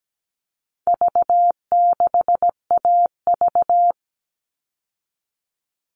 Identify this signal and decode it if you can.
Morse code, heard as a keyed tone, about 17 wpm, 700 Hz. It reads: V6AV
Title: All Cut Numbers - 4-Digit 17wpm